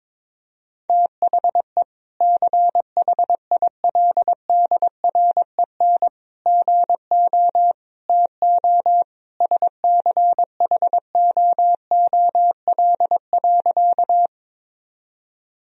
Morse code, heard as a keyed tone, about 22 wpm, 700 Hz. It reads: THE CHILDREN GO TO SCHOOL.
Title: Sentences from Top 1000 Words - Review Part 1 - 22wpm